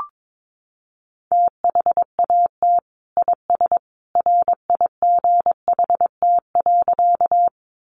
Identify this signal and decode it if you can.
Morse code, heard as a keyed tone, about 22 wpm, 700 Hz. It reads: THAT IS RIGHT.
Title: Sentences from Top 200 Words 22wpm